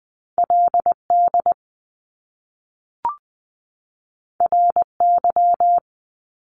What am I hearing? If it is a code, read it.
Morse code, heard as a keyed tone, about 20 wpm, 700 Hz. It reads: LD RY
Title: Top Two Letter Combinations in Words 20wpm